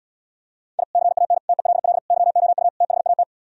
Morse code, heard as a keyed tone, about 77 wpm, 700 Hz. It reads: I CAN TELL YOU THAT